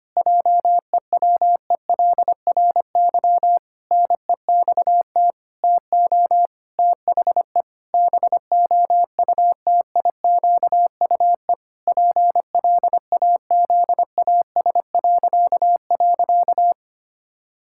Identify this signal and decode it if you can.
Morse code, heard as a keyed tone, about 25 wpm, 700 Hz. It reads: JEWELRY NEXT TO THE BOUTIQUE PLAZAS..
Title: Pangram Challenge v2 - 25wpm